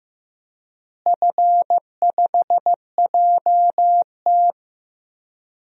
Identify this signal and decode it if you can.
Morse code, heard as a keyed tone, about 16 wpm, 700 Hz. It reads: F5JT
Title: DXCC-100 Call Signs - 4 Characters - Repeated 3x - 15wpm